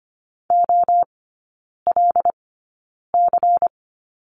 Morse code, heard as a keyed tone, about 25 wpm, spaced as for 9 wpm, 700 Hz. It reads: OLC
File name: course / 25:10wpm / 093 Lesson 16e - Letter c - Three Characters 25wpm